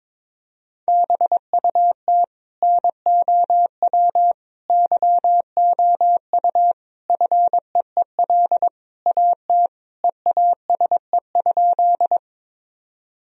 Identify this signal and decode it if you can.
Morse code, heard as a keyed tone, about 22 wpm, 700 Hz. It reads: BUT NOW YOU FEEL AT EASE?
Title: Sentences from Top 1000 Words - Review Part 1 - 22wpm